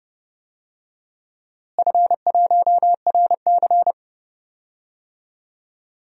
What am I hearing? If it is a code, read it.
Morse code, heard as a keyed tone, about 30 wpm, 700 Hz. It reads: F1RC
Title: DXCC-100 Call Signs - 4 Characters - Repeated 3x - 30wpm